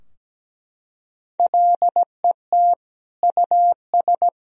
Morse code, heard as a keyed tone, about 17 wpm, 700 Hz. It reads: LET US
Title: Sentences from Top 1000 Words - Review Part 3 - 17wpm